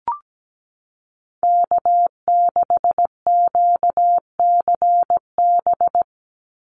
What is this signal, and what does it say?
Morse code, heard as a keyed tone, about 17 wpm, 700 Hz. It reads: K6QCB
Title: NA SOTA Chaser Callsigns 17wpm